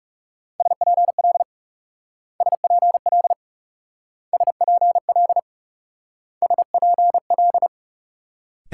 Morse code, heard as a keyed tone, about 36 wpm, 700 Hz. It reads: SPL SPL SPL SPL
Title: Speed-Racing Multi-letter Phonemes - 30wpm